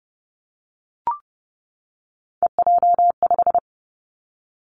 Morse code, heard as a keyed tone, about 30 wpm, 700 Hz. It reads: EJ5
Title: Three Challenging Letter-Number Combinations 30wpm